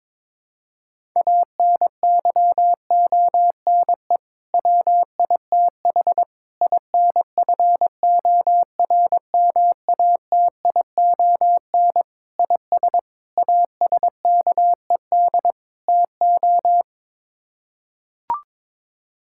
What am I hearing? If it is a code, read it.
Morse code, heard as a keyed tone, about 22 wpm, 700 Hz. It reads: ANYONE WITH INFORMATION IS ASKED TO
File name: practice / 22wpm / Sets of 6 Words - 22wpm